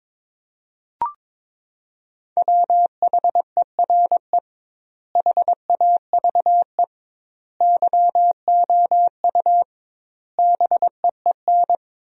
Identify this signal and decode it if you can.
Morse code, heard as a keyed tone, about 22 wpm, 700 Hz. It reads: WHERE HAVE YOU BEEN
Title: 2x Word-Spacing - Sentences from Top 300 Words - 22wpm